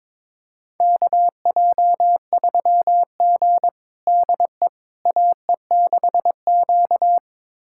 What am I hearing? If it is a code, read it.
Morse code, heard as a keyed tone, about 22 wpm, 700 Hz. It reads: KJ3G DE AE6Q
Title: USA Callsign DE Callsign - 4 Characters - 22wpm